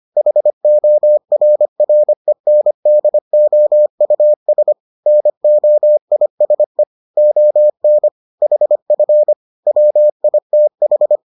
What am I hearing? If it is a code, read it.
Morse code, heard as a keyed tone, about 25 wpm, 600 Hz. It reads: HORRENDOUS NOISE ON HF WITH